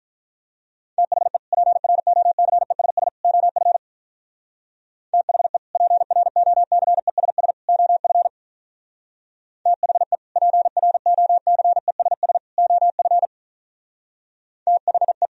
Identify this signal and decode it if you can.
Morse code, heard as a keyed tone, about 45 wpm, 700 Hz. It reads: THE PROCESS OF THE PROCESS OF THE PROCESS OF THE